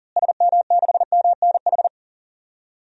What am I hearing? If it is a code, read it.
Morse code, heard as a keyed tone, about 39 wpm, 700 Hz. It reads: SM6MNH
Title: DXCC-100 Call Signs - 6 Characters - Repeated 2x - 40wpm